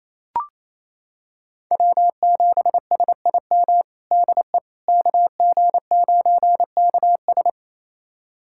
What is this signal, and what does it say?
Morse code, heard as a keyed tone, about 28 wpm, 700 Hz. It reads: W7SIM DE KG9KS
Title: USA Callsign DE Callsign - 5 Characters - 28wpm